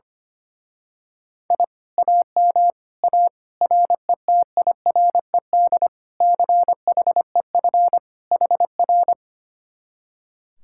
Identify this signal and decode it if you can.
Morse code, heard as a keyed tone, about 25 wpm, 700 Hz. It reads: I AM A RETIRED CHEF HR
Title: Speed-Racing Common QSO Phrases - 25wpm